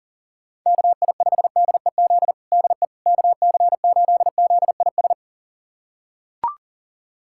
Morse code, heard as a keyed tone, about 37 wpm, 700 Hz. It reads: KI5DEZ DE KC8ZIS
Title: USA Callsign DE Callsign 40wpm